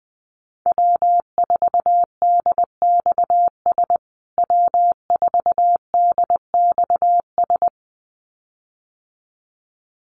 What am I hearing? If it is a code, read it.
Morse code, heard as a keyed tone, about 20 wpm, 700 Hz. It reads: W4DXS W4DXS
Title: USA Call Signs Repeated 2x 20wpm